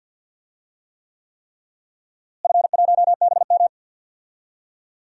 Morse code, heard as a keyed tone, about 51 wpm, 700 Hz. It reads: F1BM
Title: DXCC-100 Call Signs - 4 Characters - Repeated 3x - 50wpm